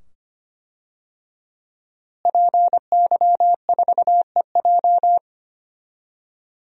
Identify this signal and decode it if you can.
Morse code, heard as a keyed tone, about 25 wpm, 700 Hz. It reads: PY4EJ
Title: DXCC-100 Call Signs - 5 Characters - Repeated 2x - 25wpm